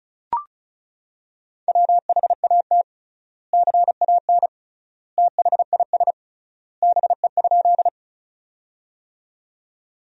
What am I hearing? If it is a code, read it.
Morse code, heard as a keyed tone, about 34 wpm, 700 Hz. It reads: WHAT CAN THIS BE?
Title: 3x Word-Spacing - Sentences from Top 100 Words - 35wpm